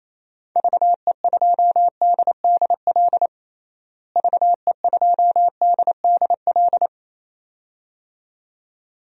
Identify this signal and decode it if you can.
Morse code, heard as a keyed tone, about 28 wpm, 700 Hz. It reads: VE2DDL VE2DDL
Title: DXCC-100 Call Signs - 6 Characters - Repeated 2x - 28wpm